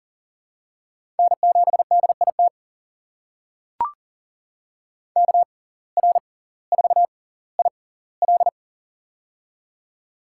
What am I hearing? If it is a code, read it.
Morse code, heard as a keyed tone, about 38 wpm, spaced as for 20 wpm, 700 Hz. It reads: N7DIT KR4IL
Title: USA Call Signs - 5 Characters - 6x Character Spacing - 40wpm